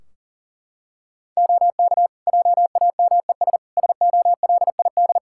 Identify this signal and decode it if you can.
Morse code, heard as a keyed tone, about 37 wpm, 700 Hz. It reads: OK JAMES SOLID
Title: Practice QSO - Exchange 3 - More Info - 40wpm